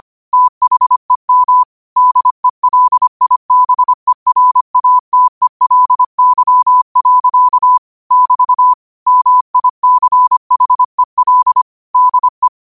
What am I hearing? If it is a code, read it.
Morse code, heard as a keyed tone, about 25 wpm, 1000 Hz. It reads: THEM DELIBERATELY. <BT> MICHEL DE
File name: qotdfiles / 25wpm